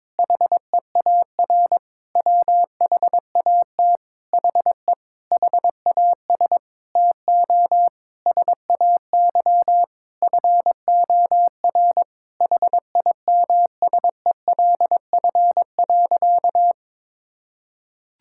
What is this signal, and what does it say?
Morse code, heard as a keyed tone, about 22 wpm, 700 Hz. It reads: HEAR WHAT HE HAS TO SAY FOR HIMSELF.
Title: Sentences from Top 1600 Words 22wpm